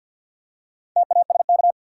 Morse code, heard as a keyed tone, about 51 wpm, 700 Hz. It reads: TASK
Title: Four Letter Words - 50wpm